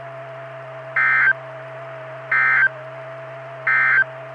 Emergency Alert System (SAME) tones, heard as harsh buzzing bursts. It is heard from 0.8 to 4.1 s.
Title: Saline, AK SKYWARN Repeater System - W5RHS